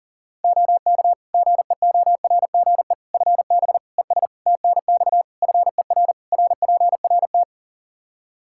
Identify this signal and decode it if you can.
Morse code, heard as a keyed tone, about 40 wpm, 700 Hz. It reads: OK GEORGE FB ES TNX FER RPRT
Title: K1USN SST QSO Practice 40wpm